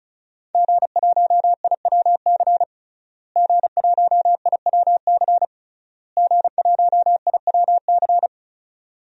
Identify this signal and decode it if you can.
Morse code, heard as a keyed tone, about 36 wpm, 700 Hz. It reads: G1IWC G1IWC G1IWC
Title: DXCC-100 Call Signs - 5 Characters - Repeated 3x - 35wpm